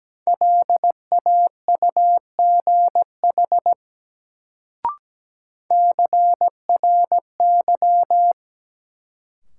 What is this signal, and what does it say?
Morse code, heard as a keyed tone, about 17 wpm, 700 Hz. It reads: LAUGH CRY
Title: Sentences from Top 500 Words - New Word Drill - 17wpm